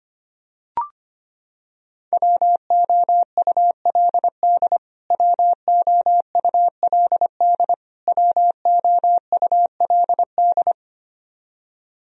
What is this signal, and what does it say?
Morse code, heard as a keyed tone, about 25 wpm, 700 Hz. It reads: WOULD WOULD WOULD
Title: Top 100 Words Repeated 25wpm